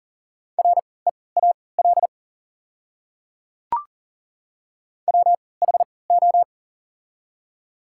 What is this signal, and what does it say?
Morse code, heard as a keyed tone, about 40 wpm, spaced as for 23 wpm, 700 Hz. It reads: REAL WHO
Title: 3x Character Spacing - Top 300 Words - 40wpm